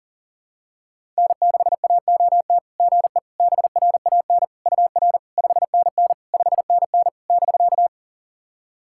Morse code, heard as a keyed tone, about 40 wpm, 660 Hz. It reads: N6AOT GE BRAN UR 5NN 5NN <BK>
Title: SOTA QSO Practice 40wpm